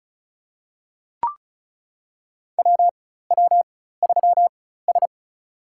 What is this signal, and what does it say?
Morse code, heard as a keyed tone, about 34 wpm, spaced as for 16 wpm, 700 Hz. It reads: WW3S
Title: USA Call Signs - 4 Characters - 4x Character Spacing - 35wpm